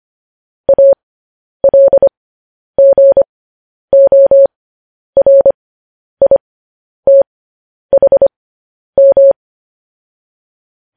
Morse code, heard as a keyed tone, about 25 wpm, spaced as for 10 wpm, 550 Hz. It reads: ALGORITHM
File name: podcast / News headlines-1x-25x10